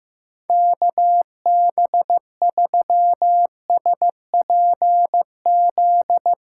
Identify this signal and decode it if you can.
Morse code, heard as a keyed tone, about 15 wpm, 700 Hz. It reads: KB3SPZ
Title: USA Callsign DE Callsign - 6 Characters - 15wpm